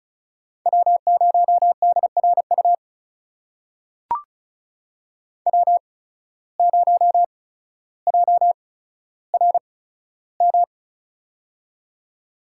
Morse code, heard as a keyed tone, about 36 wpm, spaced as for 15 wpm, 690 Hz. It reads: W0DRU W0JRM
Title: USA Call Signs - 5 Characters - 8x Character Spacing - 35wpm